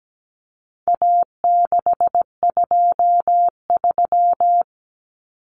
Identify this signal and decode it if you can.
Morse code, heard as a keyed tone, about 17 wpm, 700 Hz. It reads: A623
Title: Common Cut Numbers - 4-Digit 17wpm